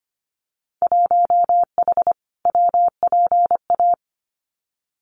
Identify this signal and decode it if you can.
Morse code, heard as a keyed tone, about 25 wpm, 690 Hz. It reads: 1H WPA
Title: Winter Field Day Exchanges 25wpm